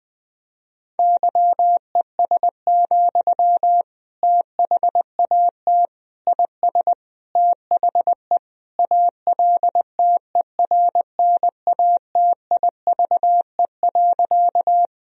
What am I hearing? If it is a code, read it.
Morse code, heard as a keyed tone, about 20 wpm, 700 Hz. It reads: YES, THAT IS THE ALTERNATIVE.
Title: Sentences from Top 1000 Words - Review Part 4 - 20wpm